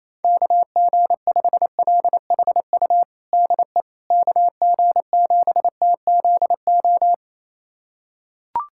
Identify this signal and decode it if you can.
Morse code, heard as a keyed tone, about 28 wpm, 700 Hz. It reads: KG5LHU DE KG7TZO
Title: USA Callsign DE Callsign - 6 Characters - 28wpm